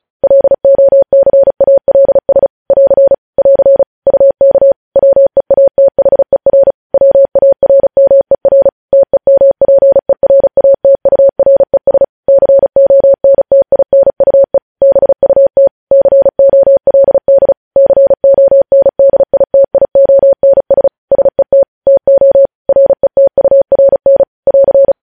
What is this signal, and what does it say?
Morse code, heard as a keyed tone, about 36 wpm, 550 Hz. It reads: LOCALS <AR> <AR> UK WEATHER WARMER TEMPERATURES CONTINUE BUT COLD CONDITIONS SET TO RETURN <AR>
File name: podcast / News headlines-1x-35x35